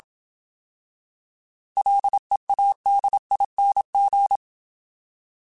Morse code, heard as a keyed tone, about 27 wpm, 800 Hz. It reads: LEADING